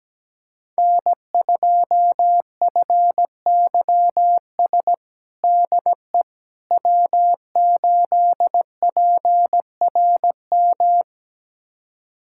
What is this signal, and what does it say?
Morse code, heard as a keyed tone, about 17 wpm, 700 Hz. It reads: N2FYS DE W8PRM
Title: USA Callsign DE Callsign - 5 Characters - 17wpm